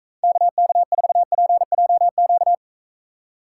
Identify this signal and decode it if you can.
Morse code, heard as a keyed tone, about 41 wpm, 700 Hz. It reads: KK4PJQ